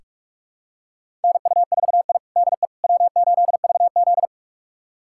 Morse code, heard as a keyed tone, about 46 wpm, 700 Hz. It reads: NU4I DE W8VZ
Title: USA Callsign DE Callsign - 4 Characters - 45wpm